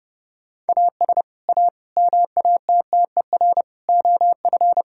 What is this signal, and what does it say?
Morse code, heard as a keyed tone, about 30 wpm, 700 Hz. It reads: AS A MATTER OF
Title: Sentences from Top 1600 Words 30wpm